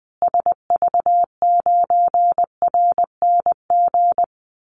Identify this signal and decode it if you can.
Morse code, heard as a keyed tone, about 20 wpm, 700 Hz. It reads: SV9RNG
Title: Call Signs Repeated 3x - 20wpm